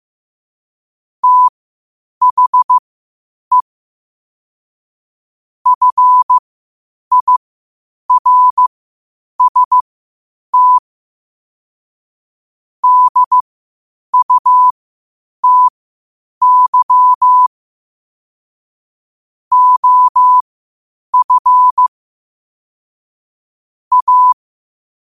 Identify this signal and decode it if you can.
Morse code, heard as a keyed tone, about 15 wpm, spaced as for 8 wpm, 1000 Hz. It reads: THE FIRST DUTY OF A
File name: qotdfiles / 8wpm